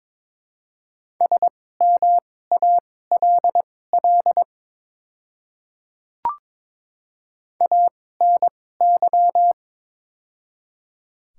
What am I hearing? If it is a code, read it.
Morse code, heard as a keyed tone, about 22 wpm, spaced as for 16 wpm, 700 Hz. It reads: SMALL ANY
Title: 2x Character Spacing - Top 200 Words - 22wpm